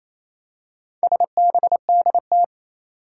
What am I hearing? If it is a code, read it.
Morse code, heard as a keyed tone, about 28 wpm, 700 Hz. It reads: SBDT
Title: Four Challenging Letter-Number Combinations 28wpm